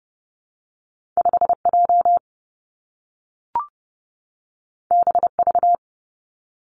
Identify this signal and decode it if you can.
Morse code, heard as a keyed tone, about 32 wpm, 700 Hz. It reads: 5 J B V
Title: Two Challenging Letter-Number Combinations 30wpm